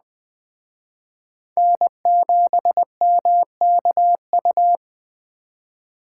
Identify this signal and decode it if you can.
Morse code, heard as a keyed tone, about 20 wpm, 700 Hz. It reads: N7MKU
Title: Speed Racing - USA 1x3 Call Signs 17wpm